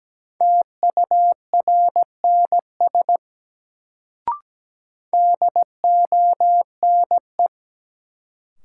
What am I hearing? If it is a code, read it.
Morse code, heard as a keyed tone, about 17 wpm, 700 Hz. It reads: TURNS DONE
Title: Sets of 5 Words Encore - New Word Drill - 17wpm